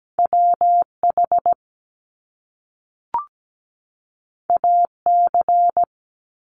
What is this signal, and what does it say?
Morse code, heard as a keyed tone, about 17 wpm, 700 Hz. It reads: WH AC
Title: Top Two Letter Combinations in Words 17wpm